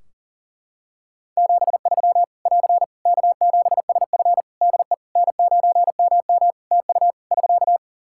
Morse code, heard as a keyed tone, about 40 wpm, 700 Hz. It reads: 73 <AR> K7SF DE N9MM TU <SK>
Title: K1USN SST QSO Practice 40wpm